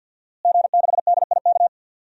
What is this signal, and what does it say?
Morse code, heard as a keyed tone, about 52 wpm, 700 Hz. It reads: G6DIK
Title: DXCC-100 Call Signs - 5 Characters - Repeated 3x - 50wpm